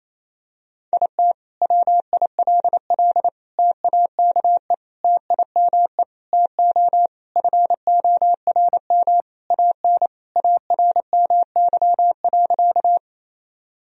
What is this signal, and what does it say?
Morse code, heard as a keyed tone, about 28 wpm, 700 Hz. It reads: IT WILL TAKE TIME TO FORM AN ARMY.
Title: Sentences from Top 1000 Words - Review Part 1 - 28wpm